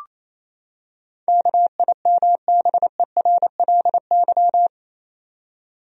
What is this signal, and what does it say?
Morse code, heard as a keyed tone, about 28 wpm, 700 Hz. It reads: KIMBERLY
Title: USA Common Names - 28wpm